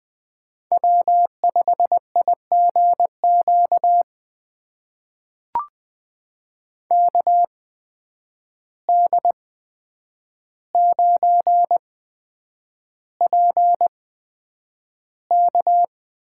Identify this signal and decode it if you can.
Morse code, heard as a keyed tone, about 20 wpm, spaced as for 9 wpm, 700 Hz. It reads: W5IGQ KD9PK
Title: USA Call Signs - 5 Characters - 8x Character Spacing - 20wpm